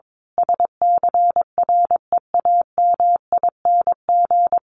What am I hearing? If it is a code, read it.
Morse code, heard as a keyed tone, about 22 wpm, 700 Hz. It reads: SCREAMING
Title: A Sound Of Thunder 22wpm